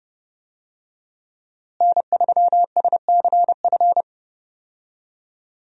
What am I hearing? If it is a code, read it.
Morse code, heard as a keyed tone, about 30 wpm, 700 Hz. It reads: N3SCF
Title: USA Call Signs - 5 Characters - Repeated 2x - 30wpm